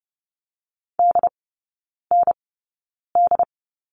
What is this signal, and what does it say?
Morse code, heard as a keyed tone, about 30 wpm, spaced as for 9 wpm, 700 Hz. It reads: DND
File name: Lesson 18e - Letter m - Three Characters 30wpm